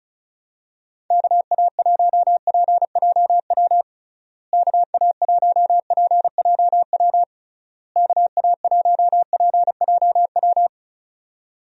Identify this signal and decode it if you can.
Morse code, heard as a keyed tone, about 34 wpm, 700 Hz. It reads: KA1PJW KA1PJW KA1PJW
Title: USA Call Signs - 6 Characters - Repeated 3x - 35wpm